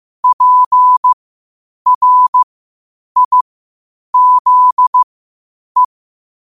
Morse code, heard as a keyed tone, about 15 wpm, spaced as for 8 wpm, 1000 Hz. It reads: PRIZE